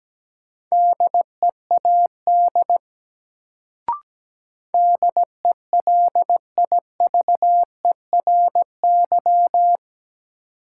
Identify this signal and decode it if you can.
Morse code, heard as a keyed tone, about 17 wpm, 700 Hz. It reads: DEAD DELIVERY
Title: Sentences from Top 1500 Words - New Word Drill Alt - 17wpm